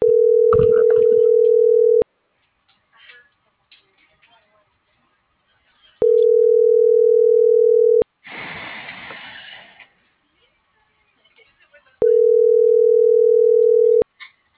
Ringback tone (North America), 440 + 480 Hz.